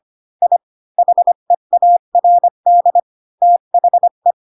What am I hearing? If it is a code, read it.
Morse code, heard as a keyed tone, about 25 wpm, 700 Hz. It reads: I HEARD THE
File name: Pinocchio / Pinocchio ch1 20 25 ik1hge